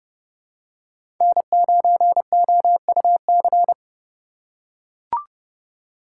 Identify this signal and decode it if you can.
Morse code, heard as a keyed tone, about 30 wpm, 700 Hz. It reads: N9OUC
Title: USA Call Signs - 5 Characters - 3x Character Spacing - 30wpm